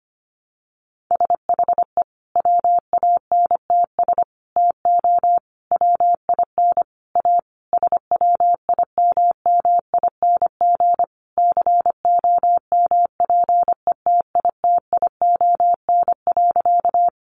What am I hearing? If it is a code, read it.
Morse code, heard as a keyed tone, about 25 wpm, 700 Hz. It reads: SHE WANTS TO WIN A SWIMMING COMPETITION.
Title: Additional Easy Phrases 25wpm